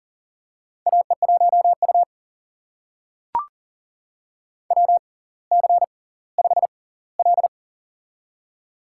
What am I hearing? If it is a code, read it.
Morse code, heard as a keyed tone, about 40 wpm, spaced as for 20 wpm, 700 Hz. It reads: AE1U WC5L
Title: USA Call Signs - 4 Characters - 6x Character Spacing - 40wpm